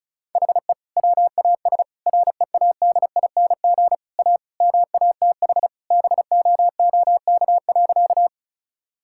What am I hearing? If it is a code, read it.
Morse code, heard as a keyed tone, about 34 wpm, 700 Hz. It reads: HE WAS READING A MATH BOOK.
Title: Sentences from Top 1000 Words - Review Part 6 - 35wpm